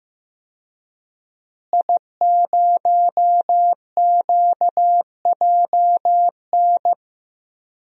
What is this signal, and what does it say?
Morse code, heard as a keyed tone, about 15 wpm, 700 Hz. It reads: I0QJN
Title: DXCC-100 Call Signs - 5 Characters - Repeated 3x - 15wpm